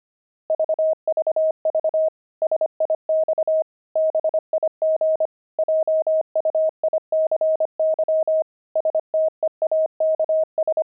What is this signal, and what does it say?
Morse code, heard as a keyed tone, about 25 wpm, 630 Hz. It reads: VVV SIX BIG JUICY STEAKS
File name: pangram-fs-25-cs-25-steaks